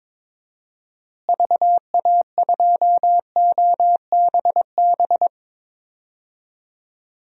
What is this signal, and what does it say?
Morse code, heard as a keyed tone, about 22 wpm, 700 Hz. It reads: VA2OBB
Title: DXCC-100 Call Signs - 6 Characters - Repeated 3x - 22wpm